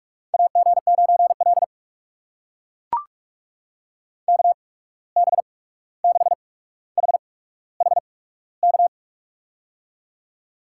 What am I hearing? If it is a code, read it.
Morse code, heard as a keyed tone, about 44 wpm, spaced as for 17 wpm, 700 Hz. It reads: AG9L KB6HHK
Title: USA Call Signs - 8x Character Spacing - 45wpm